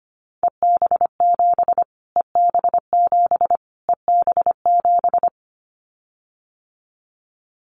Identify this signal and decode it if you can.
Morse code, heard as a keyed tone, about 25 wpm, 700 Hz. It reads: EB7 EB7 EB7
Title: Callsign Prefixes for Top 25 DXCC - Repeated 3x - 25wpm